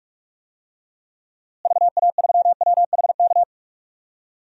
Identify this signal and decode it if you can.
Morse code, heard as a keyed tone, about 46 wpm, 700 Hz. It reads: VA3WHK
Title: DXCC-100 Call Signs - 6 Characters - Repeated 2x - 45wpm